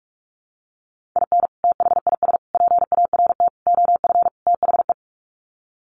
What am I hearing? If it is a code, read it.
Morse code, heard as a keyed tone, about 44 wpm, 700 Hz. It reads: IN THIS PART OF THE
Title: Sets of 5 Words 45wpm